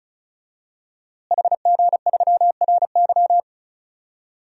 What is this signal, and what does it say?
Morse code, heard as a keyed tone, about 36 wpm, 700 Hz. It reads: HG3RY
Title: DXCC-100 Call Signs - 5 Characters - Repeated 3x - 35wpm